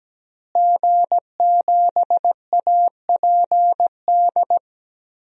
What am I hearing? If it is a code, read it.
Morse code, heard as a keyed tone, about 17 wpm, 700 Hz. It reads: G7APD
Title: DXCC-100 Call Signs - 5 Characters - Repeated 3x - 17wpm